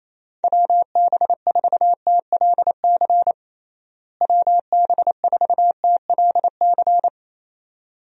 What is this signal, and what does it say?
Morse code, heard as a keyed tone, about 27 wpm, 700 Hz. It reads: WB4TLC WB4TLC
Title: USA Call Signs - 6 Characters - Repeated 2x - 28wpm